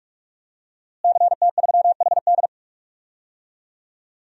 Morse code, heard as a keyed tone, about 46 wpm, 700 Hz. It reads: CT3HD
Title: DXCC-100 Call Signs - 5 Characters - Repeated 2x - 45wpm